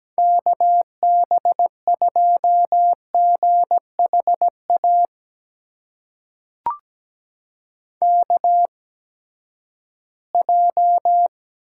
Morse code, heard as a keyed tone, about 17 wpm, spaced as for 12 wpm, 700 Hz. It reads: KB2GHA KJ
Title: USA Call Signs - 6 Characters - 8x Character Spacing - 17wpm